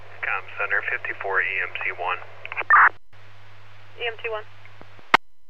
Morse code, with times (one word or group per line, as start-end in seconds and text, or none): none